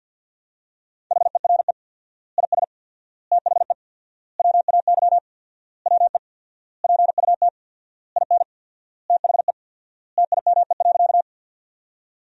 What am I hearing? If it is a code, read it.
Morse code, heard as a keyed tone, about 50 wpm, 700 Hz. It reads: HERE IS THE WAY WE PUT IN THE TIME.
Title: 4x Word-Spacing - Sentences from Top 200 Words - 50wpm